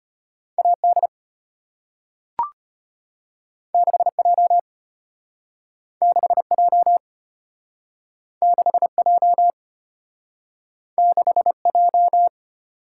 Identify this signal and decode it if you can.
Morse code, heard as a keyed tone, about 33 wpm, 700 Hz. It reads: AD 6J 6J 6J 6J